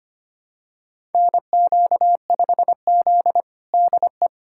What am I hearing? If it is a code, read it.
Morse code, heard as a keyed tone, about 25 wpm, 700 Hz. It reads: NQ5Z DE
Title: USA Callsign DE Callsign - 4 Characters - 25wpm